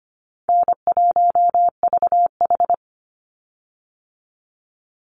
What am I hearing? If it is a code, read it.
Morse code, heard as a keyed tone, about 25 wpm, 700 Hz. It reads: N1VH
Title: Easy Call Signs - 25wpm